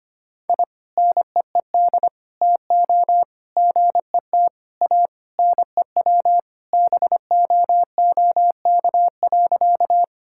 Morse code, heard as a keyed tone, about 25 wpm, 700 Hz. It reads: I NEED TO GET A NEW BOOK.
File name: practice / 25wpm / Sentences with 3-Word Phrases - 25wpm